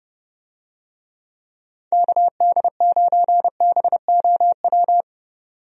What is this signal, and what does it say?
Morse code, heard as a keyed tone, about 30 wpm, 700 Hz. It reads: KD9BOW
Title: USA Call Signs - 6 Characters - 30wpm